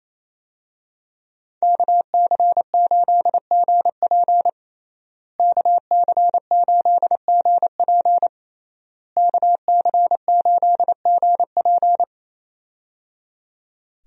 Morse code, heard as a keyed tone, about 28 wpm, 700 Hz. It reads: KC8GP KC8GP KC8GP
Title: USA Call Signs - 5 Characters - Repeated 3x - 28wpm